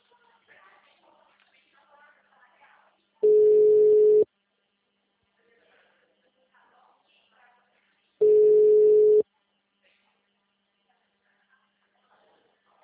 Ringback tone (Europe, 425 Hz).